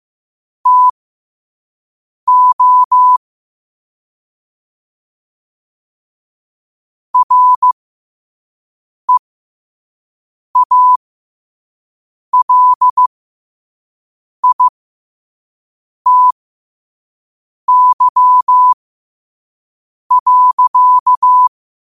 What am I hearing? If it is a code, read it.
Morse code, heard as a keyed tone, about 15 wpm, spaced as for 5 wpm, 1000 Hz. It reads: TO REALITY.